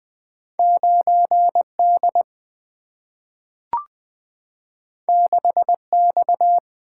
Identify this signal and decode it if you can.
Morse code, heard as a keyed tone, about 20 wpm, 700 Hz. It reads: 9D 6X
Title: Two Letter-Number Combinations - 20wpm